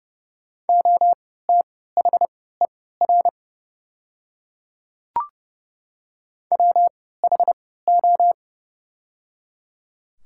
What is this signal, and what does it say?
Morse code, heard as a keyed tone, about 30 wpm, spaced as for 17 wpm, 700 Hz. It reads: OTHER WHO
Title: 3x Character Spacing - Top 100 Words - 30wpm